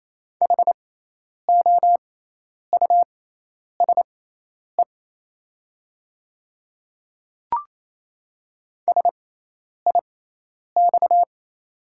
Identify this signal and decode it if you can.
Morse code, heard as a keyed tone, about 28 wpm, spaced as for 10 wpm, 690 Hz. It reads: HOUSE SIX
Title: 6x Character Spacing - Top 500 Words - 28wpm